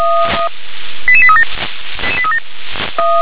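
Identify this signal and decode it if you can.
Five-tone selective call (ZVEI), heard as quick runs of five tones, tones of about 70 ms. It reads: 88837
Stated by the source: ZVEI five-tone alert received by SDR.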